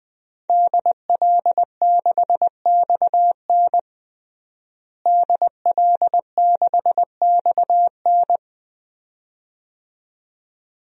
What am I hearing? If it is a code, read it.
Morse code, heard as a keyed tone, about 20 wpm, 700 Hz. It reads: DL6XN DL6XN
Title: DXCC-100 Call Signs - 5 Characters - Repeated 2x - 20wpm